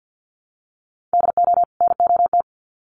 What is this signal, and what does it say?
Morse code, heard as a keyed tone, about 50 wpm, 700 Hz. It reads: DO NOT
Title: Sets of 2 Words 50wpm